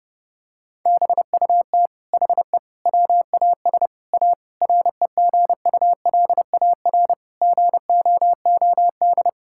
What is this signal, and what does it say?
Morse code, heard as a keyed tone, about 28 wpm, 700 Hz. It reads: BUT HE WAS A REGULAR GOOD
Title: Sentences from Top 1000 Words - Review Part 6 - 30wpm